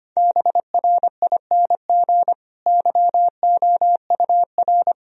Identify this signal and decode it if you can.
Morse code, heard as a keyed tone, about 25 wpm, 700 Hz. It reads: BRING YOUR
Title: Sentences from Top 1000 Words - Review Part 3 - 25wpm